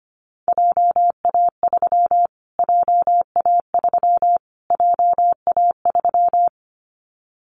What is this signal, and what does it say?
Morse code, heard as a keyed tone, about 25 wpm, 700 Hz. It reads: JA3 JA3 JA3
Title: Callsign Prefixes for Top 25 DXCC - Repeated 3x - 25wpm